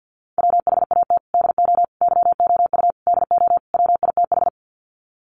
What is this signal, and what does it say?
Morse code, heard as a keyed tone, about 50 wpm, 700 Hz. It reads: WHAT DO YOU DO WITH
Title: Sets of 5 Words Encore2 50wpm